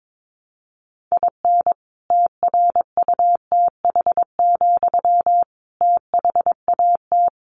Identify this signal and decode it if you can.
Morse code, heard as a keyed tone, about 22 wpm, 700 Hz. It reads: IN TRUTH, THAT
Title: Sentences from Top 1300 Words 22wpm